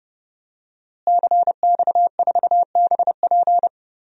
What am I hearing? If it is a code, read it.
Morse code, heard as a keyed tone, about 30 wpm, 700 Hz. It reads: CX4BP
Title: DXCC-100 Call Signs - 5 Characters - Repeated 3x - 30wpm